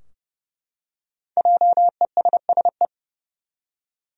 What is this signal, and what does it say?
Morse code, heard as a keyed tone, about 32 wpm, 700 Hz. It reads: JESSE
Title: USA Common Names - Max 5 Letters - 30wpm